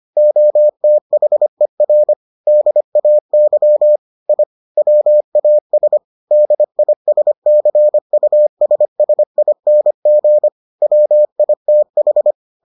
Morse code, heard as a keyed tone, about 25 wpm, 600 Hz. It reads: OTHER DAY I WAS DISCUSSING WITH